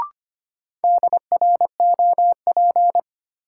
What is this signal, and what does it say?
Morse code, heard as a keyed tone, about 25 wpm, 700 Hz. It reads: DROP
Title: Sentences from Top 1400 Words - New Word Drill - 25wpm